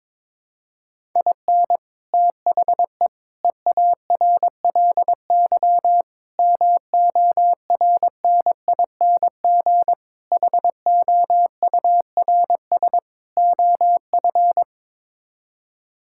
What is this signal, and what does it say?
Morse code, heard as a keyed tone, about 22 wpm, 700 Hz. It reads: IN THE EARLY MORNING HOURS OF
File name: practice / 22wpm / Sets of 6 Words - 22wpm